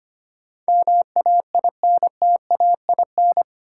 Morse code, heard as a keyed tone, about 25 wpm, 700 Hz. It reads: MAINTAIN
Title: Sentences from Top 1000 Words - Review Part 5 - New Word Drill - 25wpm